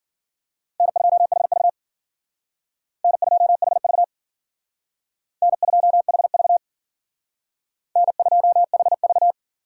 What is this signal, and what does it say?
Morse code, heard as a keyed tone, about 51 wpm, 700 Hz. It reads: N2HV N2HV N2HV N2HV